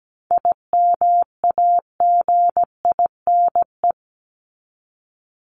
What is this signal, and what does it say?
Morse code, heard as a keyed tone, about 17 wpm, 700 Hz. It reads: IMAGINE
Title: Top 1251-1500 Words 17wpm